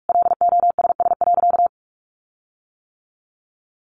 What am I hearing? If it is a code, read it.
Morse code, heard as a keyed tone, about 44 wpm, 700 Hz. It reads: LOSS.
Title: Sentences from Top 1400 Words 45wpm